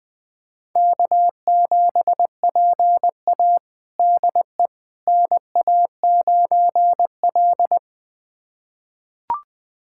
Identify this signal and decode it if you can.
Morse code, heard as a keyed tone, about 20 wpm, 700 Hz. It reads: K7PA DE NA9L
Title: USA Callsign DE Callsign - 4 Characters - 20wpm